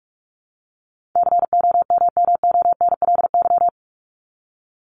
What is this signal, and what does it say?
Morse code, heard as a keyed tone, about 44 wpm, 700 Hz. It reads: COMMONLY
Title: LY Suffix - Top 100 Words with Base Words 45wpm